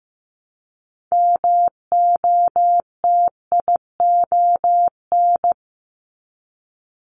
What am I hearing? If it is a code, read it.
Morse code, heard as a keyed tone, about 15 wpm, 700 Hz. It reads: MOTION
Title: Multi-letter Phonemes Repeated 3x 15wpm